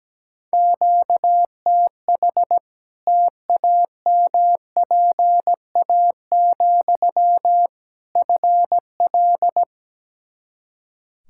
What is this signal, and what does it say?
Morse code, heard as a keyed tone, about 17 wpm, 700 Hz. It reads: QTH TAMPA, FL
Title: Practice QSO - Exchange 1 - Beginning - 17wpm